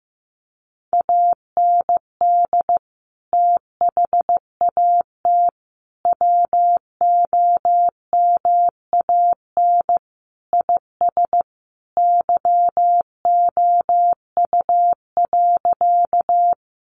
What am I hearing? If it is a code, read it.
Morse code, heard as a keyed tone, about 15 wpm, 700 Hz. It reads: AND THAT WOMAN IS YOU.
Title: Sentences from Top 1100 Words 15wpm